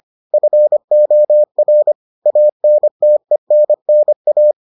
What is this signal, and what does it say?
Morse code, heard as a keyed tone, about 25 wpm, 600 Hz. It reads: FOR ANTENNA